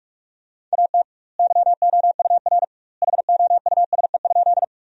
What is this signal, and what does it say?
Morse code, heard as a keyed tone, about 41 wpm, 700 Hz. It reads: AT YOUR HOUSE?
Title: 2x Word-Spacing - Sentences from Top 200 Words - 45wpm